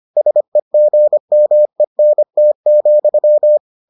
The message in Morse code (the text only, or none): SEGMENT,